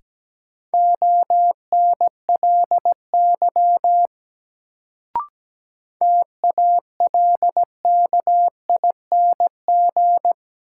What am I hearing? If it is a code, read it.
Morse code, heard as a keyed tone, about 17 wpm, 700 Hz. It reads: ONLY TALKING
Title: Sets of 2 Words - New Word Drill - 17wpm